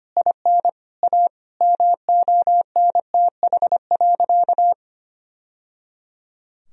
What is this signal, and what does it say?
Morse code, heard as a keyed tone, about 25 wpm, 700 Hz. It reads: IN A MONTH.
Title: Sentences from Top 1000 Words - Review Part 3 - 25wpm